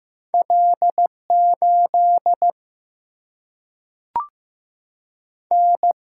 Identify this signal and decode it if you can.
Morse code, heard as a keyed tone, about 15 wpm, 700 Hz. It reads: L 8 N
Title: Single-Letter Single-Number 15wpm